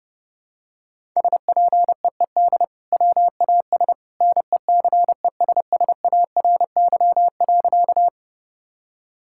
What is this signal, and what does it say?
Morse code, heard as a keyed tone, about 32 wpm, 700 Hz. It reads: SPEED WAS NECESSARY.
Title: Sentences from Top 1000 Words - Review Part 5 - 30wpm